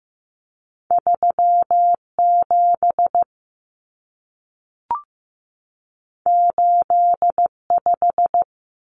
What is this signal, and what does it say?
Morse code, heard as a keyed tone, about 15 wpm, 700 Hz. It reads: 3 7 8 5
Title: Numbers - 2-Digit 15wpm